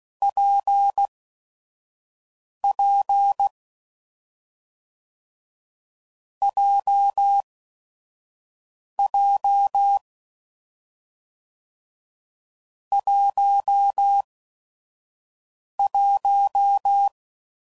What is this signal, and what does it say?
Morse code, heard as a keyed tone, about 16 wpm, spaced as for 5 wpm, 800 Hz. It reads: PP JJ 11
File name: Combinations / Difficult Combos / WPJ WPJ1